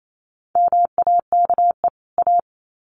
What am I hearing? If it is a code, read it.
Morse code, heard as a keyed tone, about 28 wpm, 700 Hz. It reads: MAKE A
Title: Sets of 2 Words 28wpm